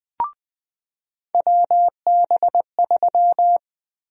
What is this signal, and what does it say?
Morse code, heard as a keyed tone, about 21 wpm, 700 Hz. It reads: WB3
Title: USA Call Sign Prefixes - 20wpm